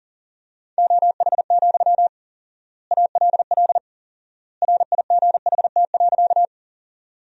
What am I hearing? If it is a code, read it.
Morse code, heard as a keyed tone, about 40 wpm, 700 Hz. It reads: OH, ALL RIGHT.
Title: 4x Word-Spacing - Sentences from Top 500 Words - 40wpm